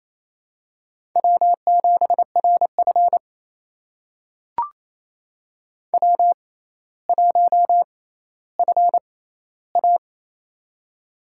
Morse code, heard as a keyed tone, about 28 wpm, spaced as for 14 wpm, 700 Hz. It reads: W7RF W1FA
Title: USA Call Signs - 4 Characters - 6x Character Spacing - 28wpm